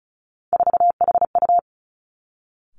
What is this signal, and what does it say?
Morse code, heard as a keyed tone, about 34 wpm, 700 Hz. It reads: 4HU
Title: Three Challenging Letter-Number Combinations 35wpm